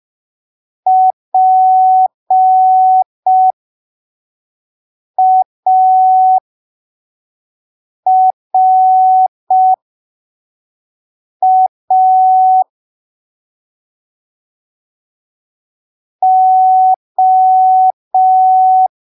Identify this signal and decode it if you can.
Morse code, heard as a keyed tone, about 5 wpm, spaced as for 3 wpm, 750 Hz. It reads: PARA O
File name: arquivos / texto / texto 09 05ppm